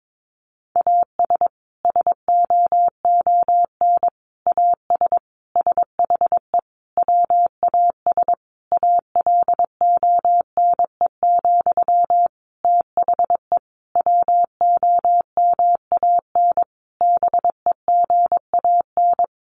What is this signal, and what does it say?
Morse code, heard as a keyed tone, about 22 wpm, 700 Hz. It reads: AS SOON AS SHE WAS ALONE, THE WOMAN BEGAN
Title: Sentences from Top 1100 Words 22wpm